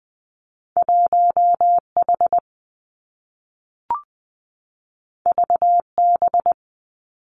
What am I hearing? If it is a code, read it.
Morse code, heard as a keyed tone, about 20 wpm, 700 Hz. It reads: 1 H V B
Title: Two Challenging Letter-Number Combinations 20wpm